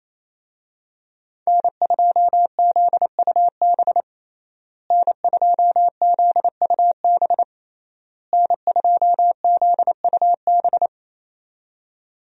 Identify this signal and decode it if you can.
Morse code, heard as a keyed tone, about 27 wpm, 700 Hz. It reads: N2ZUB N2ZUB N2ZUB
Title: USA Call Signs - 5 Characters - Repeated 3x - 28wpm